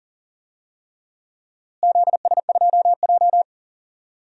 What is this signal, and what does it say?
Morse code, heard as a keyed tone, about 40 wpm, 700 Hz. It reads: ZS2J
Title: DXCC-100 Call Signs - 4 Characters - Repeated 3x - 40wpm